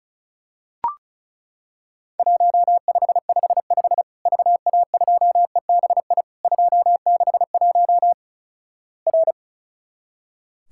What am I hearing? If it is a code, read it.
Morse code, heard as a keyed tone, about 36 wpm, 700 Hz. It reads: 1555 VA2EBI 261 R
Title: Worked All Europe DX Contest - QTC Exchange - 35wpm